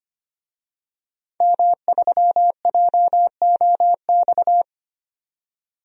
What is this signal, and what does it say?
Morse code, heard as a keyed tone, about 25 wpm, 700 Hz. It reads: M3JOX
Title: DXCC-100 Call Signs - 5 Characters - 25wpm